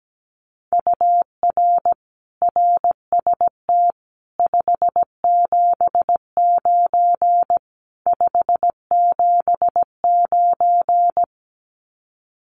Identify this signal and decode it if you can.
Morse code, heard as a keyed tone, about 17 wpm, 700 Hz. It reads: UR RST 579 579
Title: Common QSO Phrases Encore 17wpm